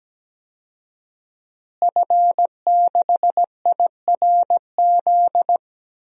Morse code, heard as a keyed tone, about 17 wpm, 700 Hz. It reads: F6IRZ
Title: DXCC-100 Call Signs - 5 Characters - Repeated 2x - 17wpm